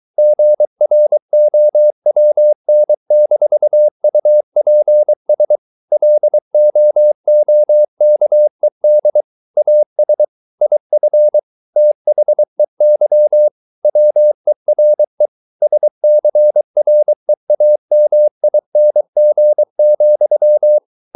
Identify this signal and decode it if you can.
Morse code, heard as a keyed tone, about 23 wpm, 600 Hz. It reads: GROWN-UPS LOOKED AS IF THEY WERE SCREAMING,